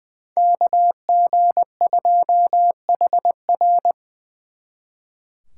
Morse code, heard as a keyed tone, about 24 wpm, 700 Hz. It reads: KG2HR
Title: Speed Racing - USA 2x2 Call Signs 20wpm